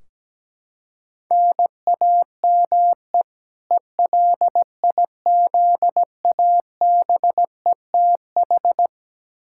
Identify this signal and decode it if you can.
Morse code, heard as a keyed tone, about 17 wpm, 700 Hz. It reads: NAME ELIZABETH
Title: K1USN SST QSO Practice 17wpm